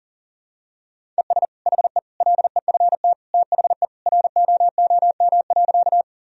Morse code, heard as a keyed tone, about 40 wpm, 700 Hz. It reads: ES HE LEFT THE ROOM.
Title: Sentences from Top 300 Words and Abbreviations 40wpm